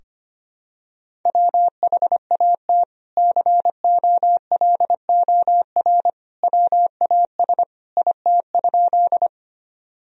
Morse code, heard as a keyed tone, about 25 wpm, 700 Hz. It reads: WHAT COLOR WAS IT?
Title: Sentences from Top 1000 Words - Review Part 2 - 25wpm